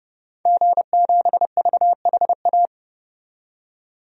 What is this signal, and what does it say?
Morse code, heard as a keyed tone, about 30 wpm, 700 Hz. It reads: G7VHA
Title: DXCC-100 Call Signs - 5 Characters - 30wpm